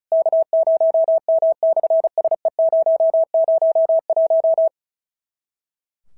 Morse code, heard as a keyed tone, about 36 wpm, 650 Hz. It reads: K0M/SE001
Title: SOTA QSO Practice 35wpm